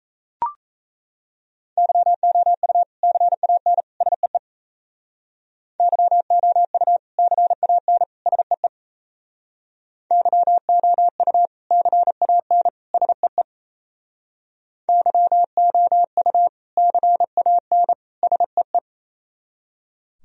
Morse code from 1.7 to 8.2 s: YOU CAN SEE YOU CAN

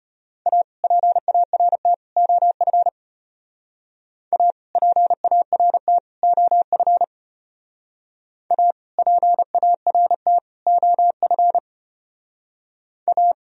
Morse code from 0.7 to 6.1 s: PART OF A PART

